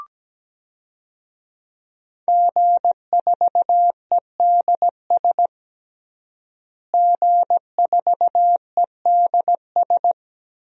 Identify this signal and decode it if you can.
Morse code, heard as a keyed tone, about 17 wpm, 700 Hz. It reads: G4EDS G4EDS
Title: DXCC-100 Call Signs - 5 Characters - Repeated 2x - 17wpm